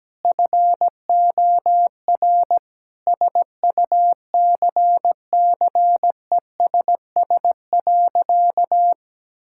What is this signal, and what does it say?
Morse code, heard as a keyed tone, about 17 wpm, 700 Hz. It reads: FOR SUCCESS.
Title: Sentences from Top 1000 Words - Review Part 5 - 17wpm